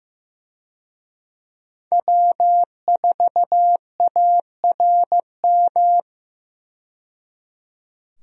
Morse code, heard as a keyed tone, about 15 wpm, 700 Hz. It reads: W4ARM
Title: USA Call Signs - 5 Characters - 15wpm